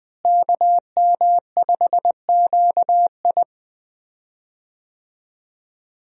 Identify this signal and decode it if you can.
Morse code, heard as a keyed tone, about 20 wpm, 700 Hz. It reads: KM5QI
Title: USA Call Signs - 5 Characters - Repeated 3x - 20wpm